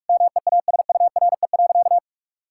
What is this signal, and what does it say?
Morse code, heard as a keyed tone, about 46 wpm, 690 Hz. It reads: MEASURE.
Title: Sentences from Top 1000 Words - Review Part 6 - 45wpm